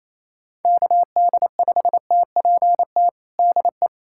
Morse code, heard as a keyed tone, about 28 wpm, 700 Hz. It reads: KD5TPT DE
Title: USA Callsign DE Callsign 28wpm